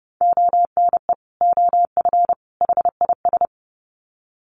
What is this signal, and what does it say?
Morse code, heard as a keyed tone, about 30 wpm, 700 Hz. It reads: ONE OF HIS
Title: Sets of 3 Words 30wpm